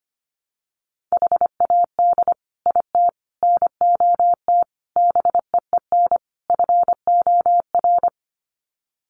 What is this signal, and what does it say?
Morse code, heard as a keyed tone, about 25 wpm, 700 Hz. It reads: HAD IT NOT BEEN FOR
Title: Sets of 5 Words Encore2 25wpm